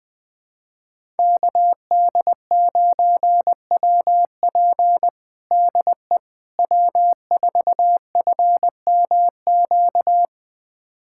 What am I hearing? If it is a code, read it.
Morse code, heard as a keyed tone, about 21 wpm, 700 Hz. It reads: KD9WP DE W4FMQ
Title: USA Callsign DE Callsign - 5 Characters - 20wpm